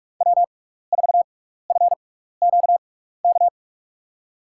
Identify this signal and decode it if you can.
Morse code, heard as a keyed tone, about 42 wpm, spaced as for 15 wpm, 700 Hz. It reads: W4FQK
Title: USA Call Signs - 6x Character Spacing - 45wpm